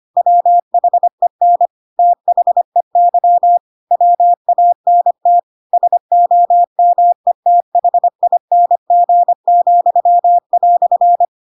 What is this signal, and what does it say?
Morse code, heard as a keyed tone, about 25 wpm, 700 Hz. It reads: WHEN THEY WANT SOMETHING,"